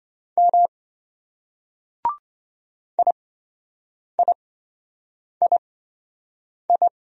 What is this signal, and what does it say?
Morse code, heard as a keyed tone, about 26 wpm, spaced as for 7 wpm, 700 Hz. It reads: M IIII